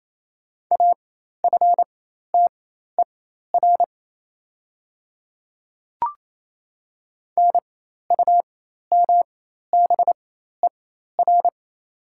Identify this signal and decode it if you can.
Morse code, heard as a keyed tone, about 27 wpm, spaced as for 13 wpm, 690 Hz. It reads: AFTER NUMBER
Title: 4x Character Spacing - Top 200 Words - 28wpm